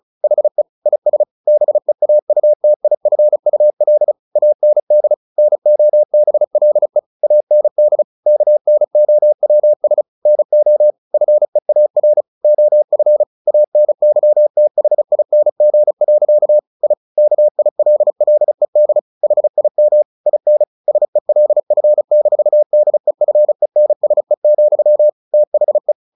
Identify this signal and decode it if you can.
Morse code, heard as a keyed tone, about 34 wpm, 600 Hz. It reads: HE IS BEAUTIFUL AND NOBLE AND KNOWS NO FEAR OF ANYTHING. I KILLED HIM IN SELF-DEFENSE, THE
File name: the old man and the sea 2 35wpm